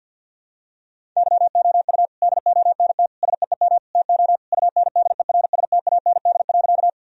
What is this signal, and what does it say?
Morse code, heard as a keyed tone, about 51 wpm, 700 Hz. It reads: YOU DONT SEEM TO UNDERSTAND.
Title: Sentences from Top 1000 Words - Review Part 1 - 50wpm